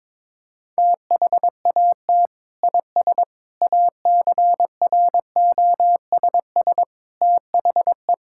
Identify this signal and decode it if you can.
Morse code, heard as a keyed tone, about 22 wpm, 700 Hz. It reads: THAT IS ACROSS THE
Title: Sentences from Top 1000 Words - Review Part 5 - 22wpm